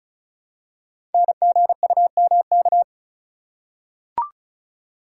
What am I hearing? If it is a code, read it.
Morse code, heard as a keyed tone, about 36 wpm, 700 Hz. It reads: NGUMK
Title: Code Groups - 5 Letters - 35wpm